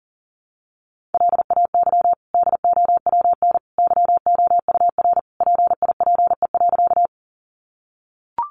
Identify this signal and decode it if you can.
Morse code, heard as a keyed tone, about 37 wpm, 700 Hz. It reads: LAY DOWN YOUR PIPE.
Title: Sentences from Top 1500 Words 40wpm